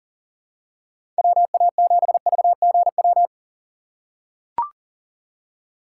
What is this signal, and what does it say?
Morse code, heard as a keyed tone, about 40 wpm, 700 Hz. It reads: WA7VGW
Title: USA Call Signs - 6 Characters - 4x Character Spacing - 40wpm